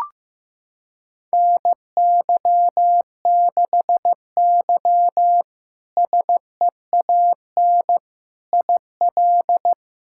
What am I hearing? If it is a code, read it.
Morse code, heard as a keyed tone, about 15 wpm, 700 Hz. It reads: NY6Y SEAN IL
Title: K1USN SST QSO Practice - 15wpm